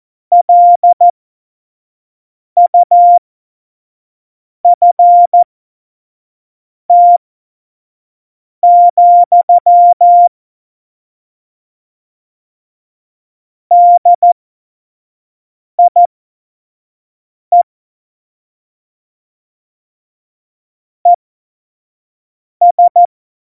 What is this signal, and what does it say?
Morse code, heard as a keyed tone, about 14 wpm, spaced as for 5 wpm, 700 Hz. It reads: LUFT, DIE ES
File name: Morsekurs / Witze / Witz 14. Mai 2019, 25 F70